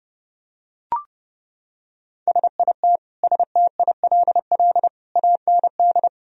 Morse code, heard as a keyed tone, about 30 wpm, 700 Hz. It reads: SIT STILL AND
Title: Sentences from Top 1000 Words - Review Part 1 - 30wpm